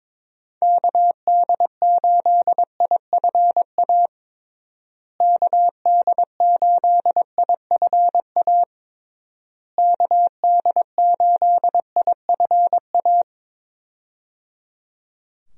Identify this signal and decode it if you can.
Morse code, heard as a keyed tone, about 22 wpm, 700 Hz. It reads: KD8IFA KD8IFA KD8IFA
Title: USA Call Signs - 6 Characters - Repeated 3x - 22wpm